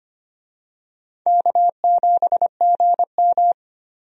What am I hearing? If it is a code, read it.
Morse code, heard as a keyed tone, about 25 wpm, 700 Hz. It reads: K7GM
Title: USA Call Signs - 4 Characters - Repeated 2x - 25wpm